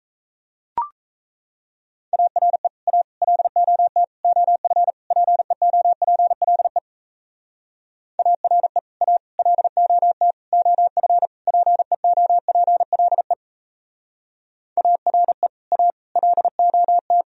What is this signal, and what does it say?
Morse code, heard as a keyed tone, about 34 wpm, 700 Hz. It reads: ARE A LOT OF PEOPLE ARE A LOT OF PEOPLE ARE A LOT
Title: Speed-Racing - Sets of 5 Words 28wpm